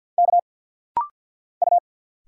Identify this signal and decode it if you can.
Morse code, heard as a keyed tone, about 50 wpm, 700 Hz. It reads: K U
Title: Single Letters - Rapid-Fire 50wpm